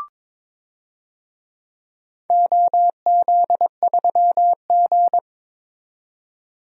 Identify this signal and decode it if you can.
Morse code, heard as a keyed tone, about 22 wpm, 700 Hz. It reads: OZ3G
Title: DXCC-100 Call Signs - 4 Characters - 22wpm